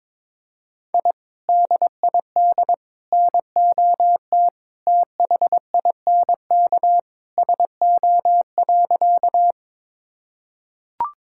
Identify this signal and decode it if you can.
Morse code, heard as a keyed tone, about 21 wpm, 700 Hz. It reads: I DID NOT THINK SO.
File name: practice / 22wpm / Sentences with 2-Word Phrases - 22wpm